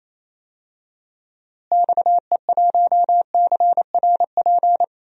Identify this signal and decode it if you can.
Morse code, heard as a keyed tone, about 28 wpm, 700 Hz. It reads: XE1CRP
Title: DXCC-100 Call Signs - 6 Characters - Repeated 3x - 28wpm